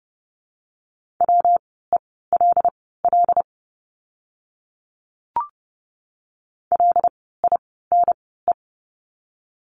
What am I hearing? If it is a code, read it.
Morse code, heard as a keyed tone, about 30 wpm, spaced as for 17 wpm, 700 Hz. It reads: WELL LINE
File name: course / 30:10wpm / 115 Lesson 19f - Letter w - Words 30wpm